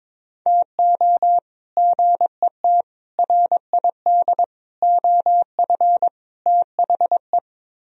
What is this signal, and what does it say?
Morse code, heard as a keyed tone, about 22 wpm, 700 Hz. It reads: TO GET RID OF THE
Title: Speed-Racing - Sets of 5 Words 22wpm